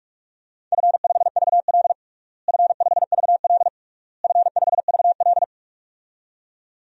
Morse code, heard as a keyed tone, about 46 wpm, 700 Hz. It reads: F5VL F5VL F5VL
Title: DXCC-100 Call Signs - 4 Characters - Repeated 3x - 45wpm